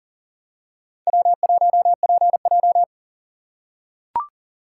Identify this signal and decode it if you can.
Morse code, heard as a keyed tone, about 40 wpm, 700 Hz. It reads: W1PJ
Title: USA Call Signs - 4 Characters - 2x Character Spacing - 40wpm